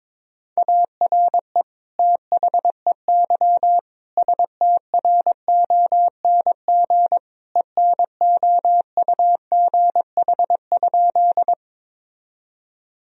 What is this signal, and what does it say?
Morse code, heard as a keyed tone, about 22 wpm, 700 Hz. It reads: ARE THEY STRONG ENOUGH?
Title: Sentences from Top 1000 Words - Review Part 3 - 22wpm